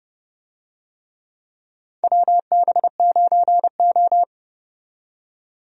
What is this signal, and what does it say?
Morse code, heard as a keyed tone, about 30 wpm, 700 Hz. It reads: WB9O
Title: USA Call Signs - 4 Characters - 30wpm